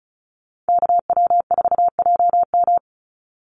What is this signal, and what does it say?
Morse code, heard as a keyed tone, about 34 wpm, 700 Hz. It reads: KW4JM
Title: NA SOTA Activator Callsigns 35wpm